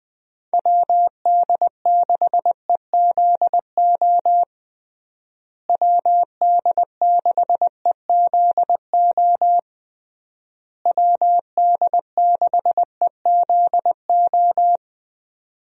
Morse code, heard as a keyed tone, about 20 wpm, 700 Hz. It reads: WD6EZO WD6EZO WD6EZO
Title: USA Call Signs - 6 Characters - Repeated 3x - 20wpm